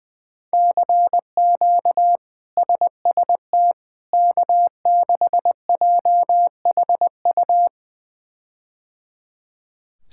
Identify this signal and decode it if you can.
Morse code, heard as a keyed tone, about 21 wpm, 700 Hz. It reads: CQ SST K6JHU
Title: K1USN SST QSO Practice - 20wpm